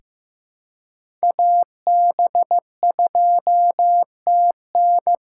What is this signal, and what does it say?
Morse code, heard as a keyed tone, about 15 wpm, 700 Hz. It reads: AB2TN
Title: USA Call Signs - 5 Characters - 2x Character Spacing - 15wpm